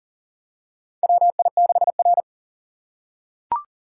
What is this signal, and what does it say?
Morse code, heard as a keyed tone, about 40 wpm, 700 Hz. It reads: WI6R
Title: USA Call Signs - 4 Characters - 6x Character Spacing - 40wpm